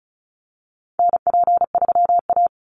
Morse code, heard as a keyed tone, about 34 wpm, 700 Hz. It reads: NP3A
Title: Call Signs Repeated 3x - 35wpm